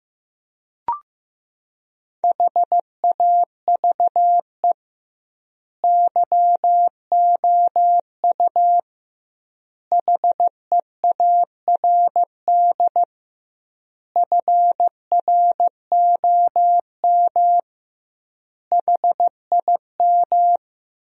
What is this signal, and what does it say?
Morse code, heard as a keyed tone, about 15 wpm, 700 Hz. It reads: HAVE YOU HEARD FROM HIM
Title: 2x Word-Spacing - Sentences from Top 400 Words - 15wpm